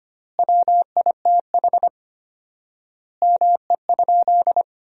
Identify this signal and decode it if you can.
Morse code, heard as a keyed tone, about 25 wpm, 700 Hz. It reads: WITH ME?
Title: 4x Word-Spacing - Sentences from Top 300 Words - 25wpm